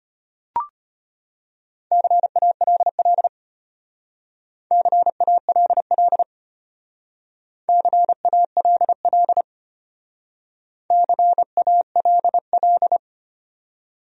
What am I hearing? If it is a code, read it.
Morse code, heard as a keyed tone, about 31 wpm, 700 Hz. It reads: CALL CALL CALL CALL
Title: Speed-Racing Top 100 Words 25wpm